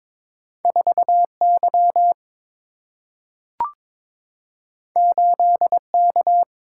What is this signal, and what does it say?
Morse code, heard as a keyed tone, about 21 wpm, 700 Hz. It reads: 4 Y 8 K
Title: Single-Letter Single-Number 22wpm